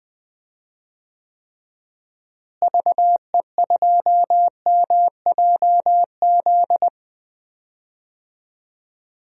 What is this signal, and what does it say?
Morse code, heard as a keyed tone, about 21 wpm, 700 Hz. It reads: VE2MJZ
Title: DXCC-100 Call Signs - 6 Characters - Repeated 2x - 20wpm